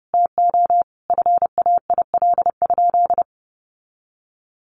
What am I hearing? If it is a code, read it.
Morse code, heard as a keyed tone, about 30 wpm, 700 Hz. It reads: TO FAIL?
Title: Sentences from Top 1400 Words 30wpm